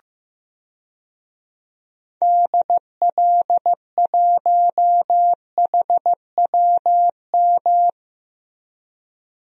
Morse code, heard as a keyed tone, about 14 wpm, 700 Hz. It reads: DL1HWM
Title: DXCC-100 Call Signs - 6 Characters - Repeated 3x - 15wpm